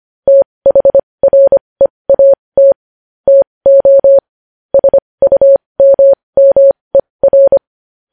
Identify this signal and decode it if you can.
Morse code, heard as a keyed tone, about 25 wpm, spaced as for 20 wpm, 550 Hz. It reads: THREAT TO SUMMER
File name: podcast / News headlines-1x-25x20